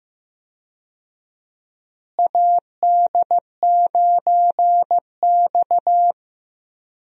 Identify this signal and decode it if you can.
Morse code, heard as a keyed tone, about 15 wpm, 700 Hz. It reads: AD9X
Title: USA Call Signs - 4 Characters - Repeated 2x - 15wpm